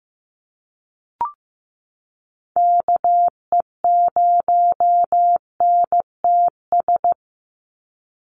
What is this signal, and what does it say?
Morse code, heard as a keyed tone, about 15 wpm, 700 Hz. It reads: KE0NTS
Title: USA Call Signs 15wpm